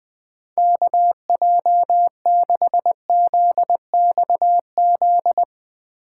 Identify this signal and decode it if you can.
Morse code, heard as a keyed tone, about 20 wpm, 700 Hz. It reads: KJ6ZXZ